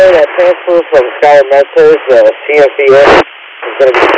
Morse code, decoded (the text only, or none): none